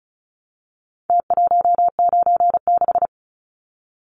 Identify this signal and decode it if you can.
Morse code, heard as a keyed tone, about 34 wpm, 700 Hz. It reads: T196
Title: Numbers - Contest Style - 4-Digit 35wpm